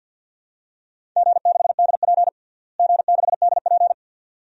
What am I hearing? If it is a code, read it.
Morse code, heard as a keyed tone, about 51 wpm, 700 Hz. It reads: G6DP G6DP
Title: DXCC-100 Call Signs - 4 Characters - Repeated 2x - 50wpm